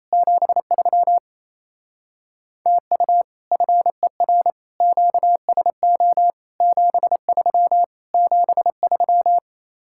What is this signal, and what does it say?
Morse code, heard as a keyed tone, about 35 wpm, 700 Hz. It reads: 73 TU FER QSO 73 73
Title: Speed-Racing Common QSO Phrases - 28wpm